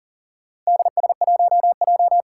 Morse code, heard as a keyed tone, about 42 wpm, 690 Hz. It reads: DS1J
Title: Four Challenging Letter-Number Combinations 40wpm